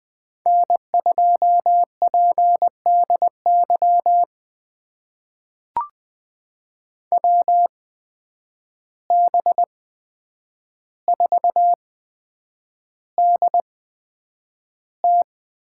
Morse code, heard as a keyed tone, about 20 wpm, spaced as for 9 wpm, 700 Hz. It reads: N2PDY WB4DT
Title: USA Call Signs - 5 Characters - 8x Character Spacing - 20wpm